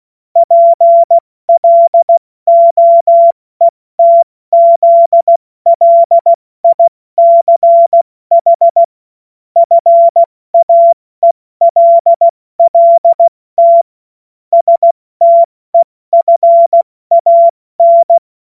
Morse code, heard as a keyed tone, about 16 wpm, spaced as for 14 wpm, 680 Hz. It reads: PLOETZLICH FAELLT STEFAN
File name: Morsekurs / Witze / Witz 30 Januar 2024 70 F80